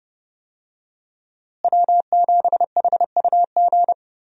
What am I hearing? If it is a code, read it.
Morse code, heard as a keyed tone, about 30 wpm, 700 Hz. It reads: W7HUG
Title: USA Call Signs - 5 Characters - Repeated 2x - 30wpm